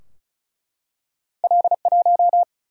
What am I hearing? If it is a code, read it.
Morse code, heard as a keyed tone, about 36 wpm, 700 Hz. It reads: L 1
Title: Single-Letter Single-Number 35wpm